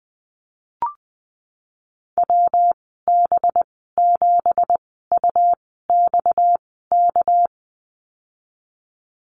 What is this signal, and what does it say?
Morse code, heard as a keyed tone, about 20 wpm, spaced as for 15 wpm, 700 Hz. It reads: WB7UXK
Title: 238 Lesson 38g - Letter q - Call Signs 20wpm